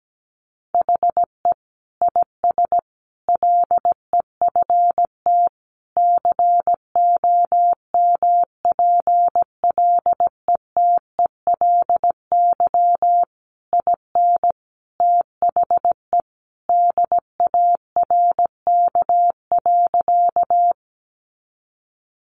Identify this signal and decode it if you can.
Morse code, heard as a keyed tone, about 17 wpm, 700 Hz. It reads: HE IS LEFT COMPLETELY IN THE DARK.
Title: Sentences from Top 1100 Words 17wpm